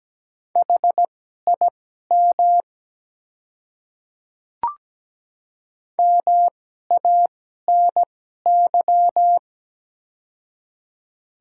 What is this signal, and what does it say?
Morse code, heard as a keyed tone, about 17 wpm, spaced as for 12 wpm, 700 Hz. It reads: HIM MANY
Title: 2x Character Spacing - Top 100 Words - 17wpm